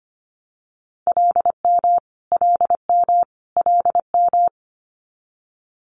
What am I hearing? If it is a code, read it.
Morse code, heard as a keyed tone, about 25 wpm, 700 Hz. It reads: LM LM LM
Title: Multi-letter Phonemes Repeated 3x 25wpm